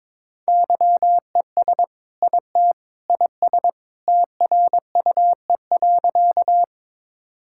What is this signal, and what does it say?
Morse code, heard as a keyed tone, about 22 wpm, 700 Hz. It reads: YES IT IS TRUE.
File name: practice / 22wpm / Sentences with 2-Word Phrases - 22wpm